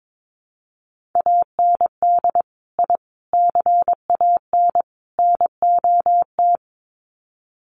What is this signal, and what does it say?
Morse code, heard as a keyed tone, about 22 wpm, 700 Hz. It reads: AND I CAN NOT
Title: Sets of 4 Words 22wpm